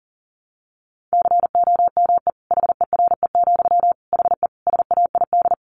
Morse code, heard as a keyed tone, about 40 wpm, 700 Hz. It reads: COME HERE, HE SAID
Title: Sentences from Top 1500 Words 40wpm